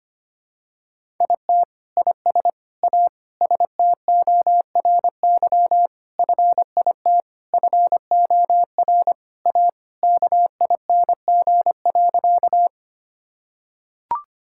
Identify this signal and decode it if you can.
Morse code, heard as a keyed tone, about 25 wpm, 700 Hz. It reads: IT IS A STORY FIT FOR A KING.
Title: Sentences from Top 1000 Words - Review Part 5 - 25wpm